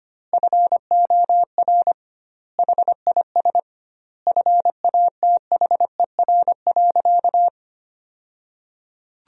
Morse code, heard as a keyed tone, about 25 wpm, 700 Hz. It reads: FOR HIS FATHER.
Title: 2x Word-Spacing - Sentences from Top 500 Words - 25wpm